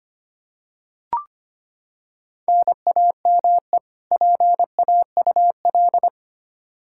Morse code, heard as a keyed tone, about 25 wpm, 700 Hz. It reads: NAME PAUL
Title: Practice QSO - Exchange 1 - Beginning - 25wpm